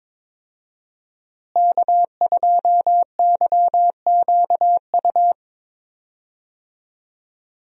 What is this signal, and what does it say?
Morse code, heard as a keyed tone, about 22 wpm, 700 Hz. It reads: K2YQU
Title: USA Call Signs - 5 Characters - Repeated 3x - 22wpm